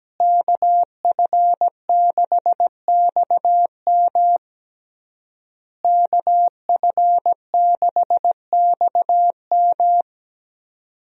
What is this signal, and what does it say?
Morse code, heard as a keyed tone, about 17 wpm, 700 Hz. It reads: KF6XM KF6XM
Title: USA Call Signs - 5 Characters - Repeated 3x - 17wpm